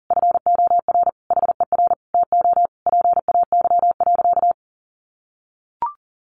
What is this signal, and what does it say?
Morse code, heard as a keyed tone, about 40 wpm, 700 Hz. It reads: FOR HER TO PAY.
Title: Sentences from Top 1500 Words 40wpm